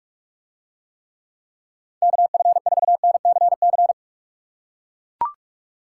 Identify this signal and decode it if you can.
Morse code, heard as a keyed tone, about 43 wpm, 700 Hz. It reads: KF4NCC